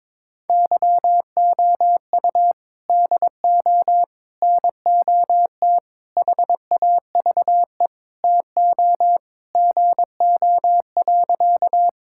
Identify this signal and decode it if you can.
Morse code, heard as a keyed tone, about 22 wpm, 700 Hz. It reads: YOU DO NOT HAVE TO GO.
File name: practice / 22wpm / Sentences with 4-Word Phrases - 22wpm